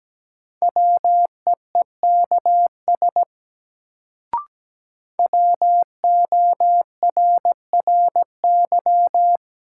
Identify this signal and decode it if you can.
Morse code, heard as a keyed tone, about 17 wpm, 700 Hz. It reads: WEEKS WORRY
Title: Sets of 5 Words - New Word Drill - 17wpm